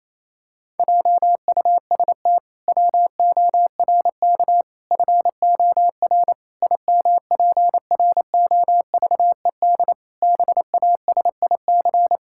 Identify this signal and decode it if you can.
Morse code, heard as a keyed tone, about 28 wpm, 700 Hz. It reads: JUST WORK FOR IMPROVED BASIC
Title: Pangram Challenge v2 - 28wpm